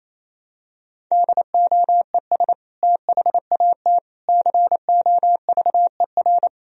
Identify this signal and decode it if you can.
Morse code, heard as a keyed tone, about 28 wpm, 700 Hz. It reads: DOES THAT COVER
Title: Sentences from Top 1000 Words - Review Part 1 - 28wpm